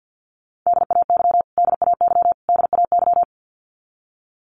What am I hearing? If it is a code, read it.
Morse code, heard as a keyed tone, about 49 wpm, 700 Hz. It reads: DAY DAY DAY
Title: Top 500 Words Repeated 50wpm